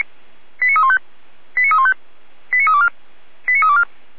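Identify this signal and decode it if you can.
Five-tone selective call (ZVEI), heard as quick runs of five tones, tones of about 70 ms. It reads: 89315 89315 89324 89324
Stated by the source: ZVEI five-tone selective call received on an SDR.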